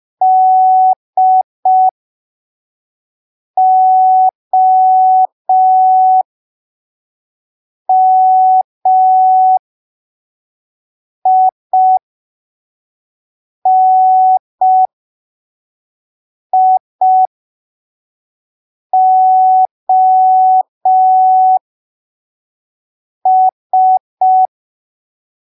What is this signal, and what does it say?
Morse code, heard as a keyed tone, about 5 wpm, spaced as for 3 wpm, 750 Hz. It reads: DOMINIOS
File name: arquivos / texto / texto 25 05ppm